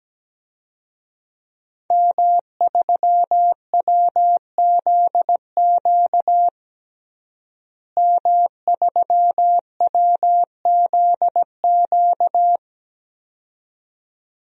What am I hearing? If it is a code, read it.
Morse code, heard as a keyed tone, about 17 wpm, 700 Hz. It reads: M3WZQ M3WZQ
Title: DXCC-100 Call Signs - 5 Characters - Repeated 2x - 17wpm